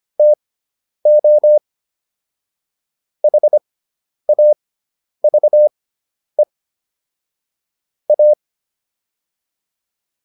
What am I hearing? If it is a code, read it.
Morse code, heard as a keyed tone, about 25 wpm, spaced as for 10 wpm, 600 Hz. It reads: TO HAVE A